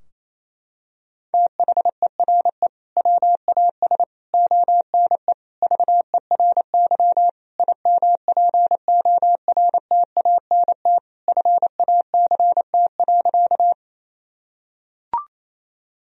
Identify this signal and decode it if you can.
Morse code, heard as a keyed tone, about 28 wpm, 700 Hz. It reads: THERE WAS ONE VERY IMPORTANT FACT.
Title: Sentences from Top 1000 Words - Review Part 3 - 28wpm